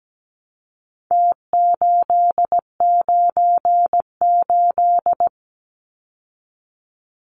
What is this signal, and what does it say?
Morse code, heard as a keyed tone, about 17 wpm, 700 Hz. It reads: T898
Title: Numbers - Contest Style - 4-Digit 17wpm